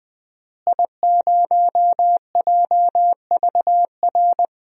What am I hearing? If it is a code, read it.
Morse code, heard as a keyed tone, about 21 wpm, 700 Hz. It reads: I0JVR
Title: DXCC-100 Call Signs - 5 Characters - Repeated 3x - 20wpm